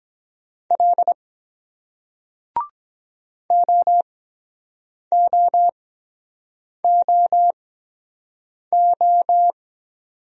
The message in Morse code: L OOOO

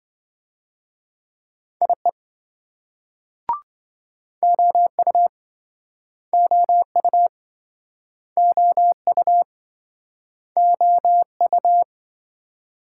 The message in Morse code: IE OU OU OU OU

